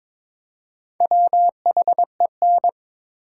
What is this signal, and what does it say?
Morse code, heard as a keyed tone, about 22 wpm, 700 Hz. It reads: WHEN
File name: Build A Phrase - 2-Word Phrases - 22wpm